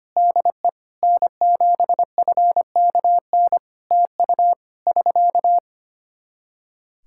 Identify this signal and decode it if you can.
Morse code, heard as a keyed tone, about 25 wpm, 700 Hz. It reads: DE N7FKN TU <SK>
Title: K1USN SST QSO Practice 25wpm